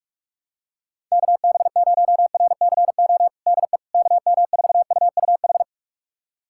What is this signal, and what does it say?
Morse code, heard as a keyed tone, about 46 wpm, 700 Hz. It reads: KB0RCO DE KM4UUH
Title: USA Callsign DE Callsign - 6 Characters - 45wpm